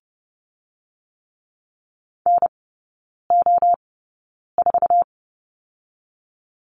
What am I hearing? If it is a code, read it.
Morse code, heard as a keyed tone, about 30 wpm, 700 Hz. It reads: N O 4
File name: Lesson 19e - Letter w - Three Characters 30wpm